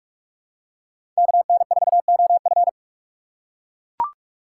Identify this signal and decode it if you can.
Morse code, heard as a keyed tone, about 44 wpm, 700 Hz. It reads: KN4OF
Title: USA Call Signs - 5 Characters - 8x Character Spacing - 45wpm